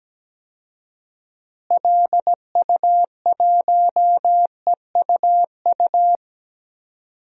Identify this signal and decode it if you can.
Morse code, heard as a keyed tone, about 17 wpm, 700 Hz. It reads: LU1EUU
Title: DXCC-100 Call Signs - 6 Characters - Repeated 2x - 17wpm